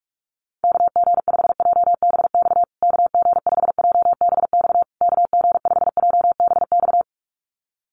Morse code, heard as a keyed tone, about 44 wpm, 700 Hz. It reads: KG5JBX KG5JBX KG5JBX
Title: USA Call Signs Repeated 3x 45wpm